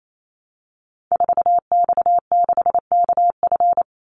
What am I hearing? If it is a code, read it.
Morse code, heard as a keyed tone, about 28 wpm, 700 Hz. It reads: 4X6KF
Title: Call Signs Repeated 2x - 28wpm